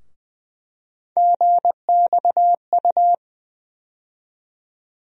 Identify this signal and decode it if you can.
Morse code, heard as a keyed tone, about 20 wpm, 700 Hz. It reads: GXU
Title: USA Call Sign Postfixes - 3 Letters 20wpm